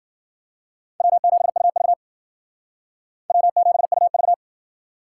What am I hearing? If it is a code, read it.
Morse code, heard as a keyed tone, about 51 wpm, 700 Hz. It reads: W7UV W7UV